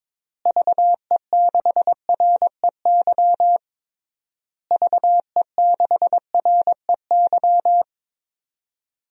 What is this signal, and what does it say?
Morse code, heard as a keyed tone, about 22 wpm, 700 Hz. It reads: VE6REY VE6REY
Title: DXCC-100 Call Signs - 6 Characters - Repeated 3x - 22wpm